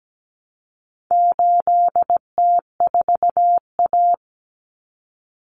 Morse code, heard as a keyed tone, about 17 wpm, 700 Hz. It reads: 8T4A
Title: Common Cut Numbers - 4-Digit 17wpm